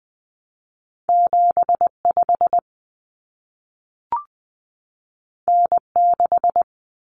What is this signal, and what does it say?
Morse code, heard as a keyed tone, about 20 wpm, 700 Hz. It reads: 75 N6
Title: Common Cut Numbers - 2-Digit 20wpm